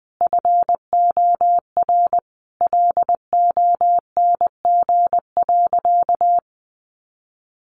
Morse code, heard as a keyed tone, about 20 wpm, 700 Hz. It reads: FOR LONG.
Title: Sentences from Top 600 Words 20wpm